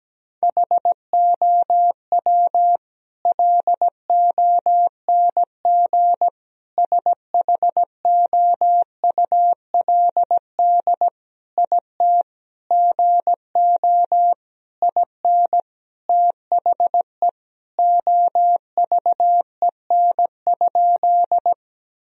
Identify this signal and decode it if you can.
Morse code, heard as a keyed tone, about 17 wpm, 700 Hz. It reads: HOW LONG SHOULD IT GO IN THE OVEN?
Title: Sentences from Top 1000 Words - Review Part 4 - 17wpm